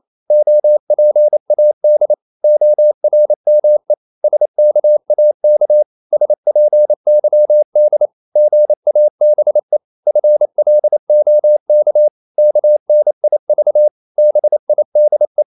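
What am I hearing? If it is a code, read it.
Morse code, heard as a keyed tone, about 28 wpm, 600 Hz. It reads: OPAD ORME SKAK SPYD GABE FLOK KNIV BIDE
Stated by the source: Danish short words sent in Morse at 28 wpm, from CW med Gnister.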